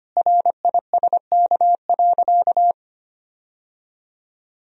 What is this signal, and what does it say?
Morse code, heard as a keyed tone, about 25 wpm, 700 Hz. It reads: RISK.
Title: Sentences from Top 1000 Words - Review Part 4 - 25wpm